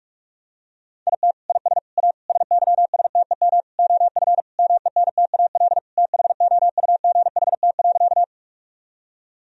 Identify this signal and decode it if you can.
Morse code, heard as a keyed tone, about 46 wpm, 700 Hz. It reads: IT IS A SYSTEM OF MENTAL THOUGHT.
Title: Sentences from Top 1000 Words - Review Part 4 - 45wpm